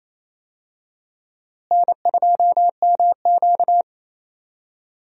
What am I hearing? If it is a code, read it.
Morse code, heard as a keyed tone, about 28 wpm, 700 Hz. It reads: N2MQ
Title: USA Call Signs - 4 Characters - Repeated 3x - 28wpm